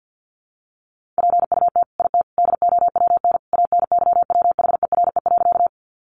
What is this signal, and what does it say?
Morse code, heard as a keyed tone, about 50 wpm, 700 Hz. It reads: PUT IT DOWN ANYWHERE.
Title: Sentences from Top 1400 Words 50wpm